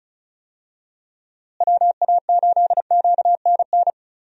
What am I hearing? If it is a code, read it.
Morse code, heard as a keyed tone, about 36 wpm, 700 Hz. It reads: WA8QNN
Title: USA Call Signs - 6 Characters - Repeated 3x - 35wpm